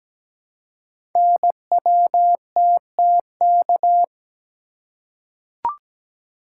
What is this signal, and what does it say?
Morse code, heard as a keyed tone, about 17 wpm, 700 Hz. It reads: NWTTK
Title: Code Groups - 5 Letters - 17wpm